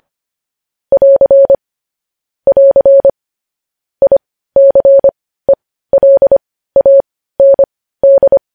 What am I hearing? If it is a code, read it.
Morse code, heard as a keyed tone, about 25 wpm, spaced as for 15 wpm, 550 Hz. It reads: <AR> <AR> ICELAND
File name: podcast / News headlines-1x-25x15